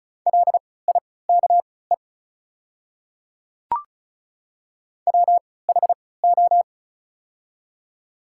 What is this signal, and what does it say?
Morse code, heard as a keyed tone, about 36 wpm, spaced as for 20 wpm, 700 Hz. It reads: LIKE WHO
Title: 3x Character Spacing - Top 100 Words - 35wpm